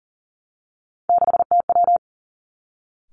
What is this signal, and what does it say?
Morse code, heard as a keyed tone, about 41 wpm, 700 Hz. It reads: 6TW
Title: Three Challenging Letter-Number Combinations 40wpm